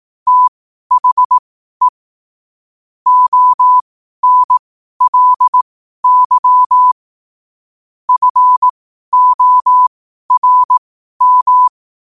Morse code, heard as a keyed tone, about 18 wpm, spaced as for 12 wpm, 1000 Hz. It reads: THE ONLY FORM
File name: qotdfiles / 12wpm